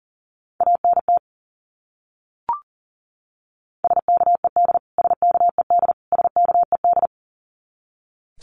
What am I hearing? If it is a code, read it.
Morse code, heard as a keyed tone, about 40 wpm, 700 Hz. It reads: ANT SKED SKED SKED
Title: Abbreviations Repeated 3x - 40wpm